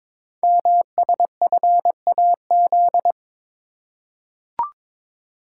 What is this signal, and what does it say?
Morse code, heard as a keyed tone, about 22 wpm, 700 Hz. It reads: MSFAZ
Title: Code Groups - 5 Letters - 22wpm